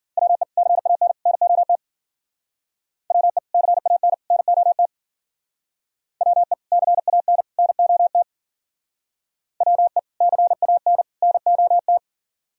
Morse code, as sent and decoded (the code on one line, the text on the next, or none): .-- . / -.-. .- -. / -. --- - / .-- . / -.-. .- -. / -. --- - / .-- . / -.-. .- -. / -. --- - / .-- . / -.-. .- -. / -. --- -
WE CAN NOT WE CAN NOT WE CAN NOT WE CAN NOT